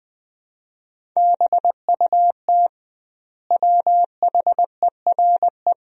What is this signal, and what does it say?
Morse code, heard as a keyed tone, about 20 wpm, 700 Hz. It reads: BUT WHERE
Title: 2x Word-Spacing - Sentences from Top 200 Words - 20wpm